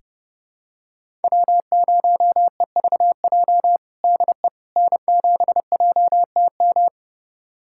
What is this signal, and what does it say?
Morse code, heard as a keyed tone, about 30 wpm, 700 Hz. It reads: W0EVJ DE N7JTM
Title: USA Callsign DE Callsign - 5 Characters - 30wpm